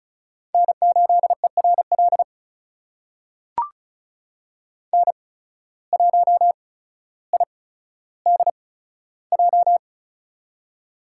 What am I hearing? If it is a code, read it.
Morse code, heard as a keyed tone, about 36 wpm, spaced as for 15 wpm, 700 Hz. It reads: N8ERL N1IDJ
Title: USA Call Signs - 8x Character Spacing - 35wpm